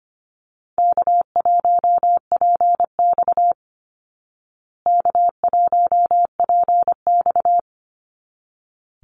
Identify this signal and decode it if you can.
Morse code, heard as a keyed tone, about 25 wpm, 700 Hz. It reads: K1PX K1PX
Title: Easy Call Signs Repeated 2x - 25wpm